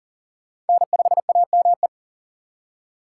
Morse code, heard as a keyed tone, about 50 wpm, 700 Hz. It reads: N5AME